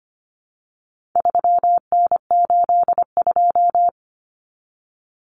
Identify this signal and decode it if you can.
Morse code, heard as a keyed tone, about 25 wpm, 700 Hz. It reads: 3N82
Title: Common Cut Numbers - 4-Digit 25wpm